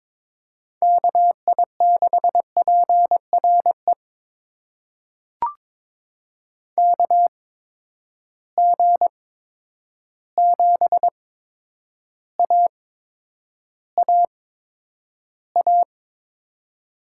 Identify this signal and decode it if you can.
Morse code, heard as a keyed tone, about 22 wpm, spaced as for 9 wpm, 700 Hz. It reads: KI6PRE KG7AAA
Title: USA Call Signs - 6 Characters - 8x Character Spacing - 22wpm